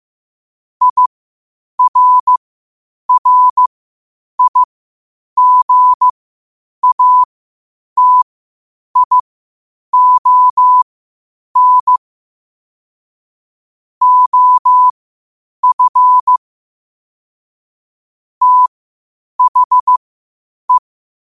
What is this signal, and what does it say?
Morse code, heard as a keyed tone, about 15 wpm, spaced as for 8 wpm, 1000 Hz. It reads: IRRIGATION OF THE
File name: qotdfiles / 8wpm